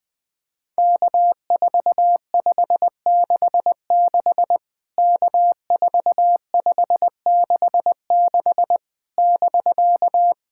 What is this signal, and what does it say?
Morse code, heard as a keyed tone, about 20 wpm, 700 Hz. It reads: K4566 K4566 <BK>
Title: POTA QSO Practice - 20wpm